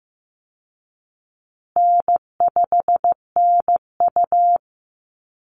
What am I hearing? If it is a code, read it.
Morse code, heard as a keyed tone, about 15 wpm, 700 Hz. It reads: N5NU
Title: Easy Call Signs Repeated 3x - 15wpm